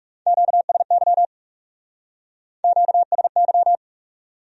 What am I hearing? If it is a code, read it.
Morse code, heard as a keyed tone, about 37 wpm, 700 Hz. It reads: QSY QSY
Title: Speed-Racing Q-Codes - 30wpm